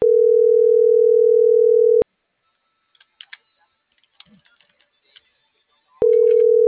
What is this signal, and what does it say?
Ringback tone (North America), 440 + 480 Hz.
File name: CENTRAL NMP / out melanie.diaznmp - .46003